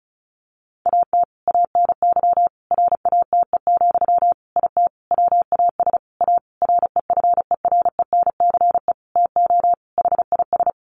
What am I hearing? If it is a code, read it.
Morse code, heard as a keyed tone, about 34 wpm, 700 Hz. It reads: AT ANY RATE, IT WAS A REFERENCE TO HIS
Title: Sentences from Top 1200 Words 35wpm